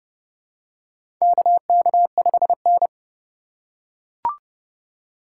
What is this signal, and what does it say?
Morse code, heard as a keyed tone, about 30 wpm, 700 Hz. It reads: KK5N
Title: USA Call Signs - 2x Character Spacing - 30wpm